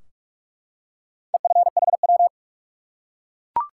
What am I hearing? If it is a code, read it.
Morse code, heard as a keyed tone, about 44 wpm, 700 Hz. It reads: EFHW
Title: Top 180 QSO Elements 45wpm